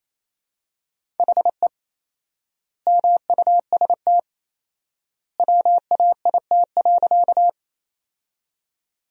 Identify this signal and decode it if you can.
Morse code, heard as a keyed tone, about 28 wpm, 700 Hz. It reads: HE MUST WAIT.
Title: 4x Word-Spacing - Sentences from Top 500 Words - 28wpm